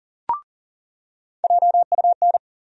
Morse code, heard as a keyed tone, about 40 wpm, 700 Hz. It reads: JUN
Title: USA Call Sign Postfixes - 3 Letters 40wpm